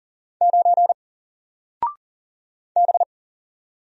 Morse code, heard as a keyed tone, about 41 wpm, 700 Hz. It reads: 9 B
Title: Single Letter-Number 40wpm